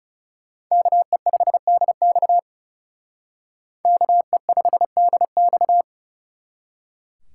Morse code, heard as a keyed tone, about 38 wpm, 700 Hz. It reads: KE5DX KE5DX